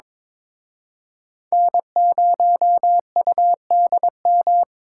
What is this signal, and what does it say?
Morse code, heard as a keyed tone, about 25 wpm, 700 Hz. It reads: N0UDM
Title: Speed Racing - USA 1x3 Call Signs 22wpm